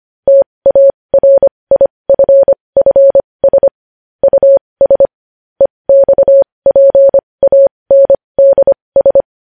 Morse code, heard as a keyed tone, about 25 wpm, spaced as for 20 wpm, 550 Hz. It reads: TARIFFS US EXPANDS
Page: News Headlines in Morse Code at 20 WPM